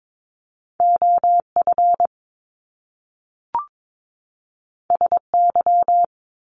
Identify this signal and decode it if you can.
Morse code, heard as a keyed tone, about 22 wpm, 700 Hz. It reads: OF SY
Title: Top Two Letter Combinations in Words 22wpm